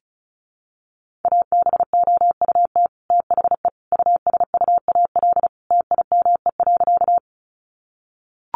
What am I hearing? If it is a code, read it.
Morse code, heard as a keyed tone, about 34 wpm, 700 Hz. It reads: ABOUT THE USUAL TIME.
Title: Sentences from Top 400 Words 35wpm